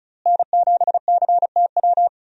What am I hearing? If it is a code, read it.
Morse code, heard as a keyed tone, about 34 wpm, 700 Hz. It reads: N7CTW
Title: USA Call Signs - 5 Characters - Repeated 3x - 35wpm